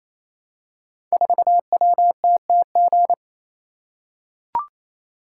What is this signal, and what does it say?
Morse code, heard as a keyed tone, about 28 wpm, 700 Hz. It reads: 4WTTG
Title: Code Groups - 5 Letter:Numbers - 28wpm